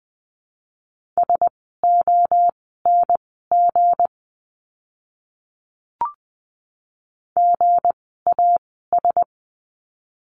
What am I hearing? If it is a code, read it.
Morse code, heard as a keyed tone, about 20 wpm, spaced as for 15 wpm, 700 Hz. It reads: SONG GAS
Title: 160 Lesson 26f - Letter g - Words 20wpm